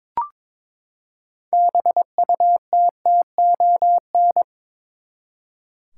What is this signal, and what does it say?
Morse code, heard as a keyed tone, about 22 wpm, 700 Hz. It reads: BUTTON
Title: Six Letter Words - 22wpm